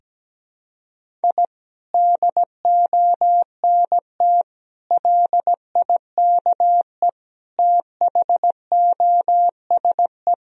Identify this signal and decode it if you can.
Morse code, heard as a keyed tone, about 17 wpm, 700 Hz. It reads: I DONT LIKE THOSE
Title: Sentences from Top 1000 Words - Review Part 1 - 17wpm